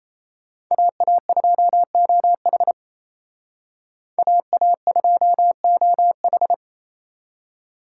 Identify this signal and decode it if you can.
Morse code, heard as a keyed tone, about 37 wpm, 700 Hz. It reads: AA2OH AA2OH